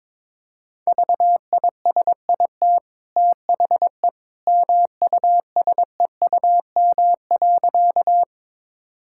Morse code, heard as a keyed tone, about 22 wpm, 690 Hz. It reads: VISIT THE MUSEUM.
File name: practice / 22wpm / Easy Sentences with Hints - 22wpm